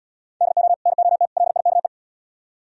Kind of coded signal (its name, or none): Morse code